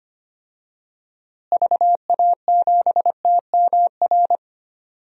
Morse code, heard as a keyed tone, about 25 wpm, 700 Hz. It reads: VA7TMR
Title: DXCC-100 Call Signs - 6 Characters - Repeated 3x - 25wpm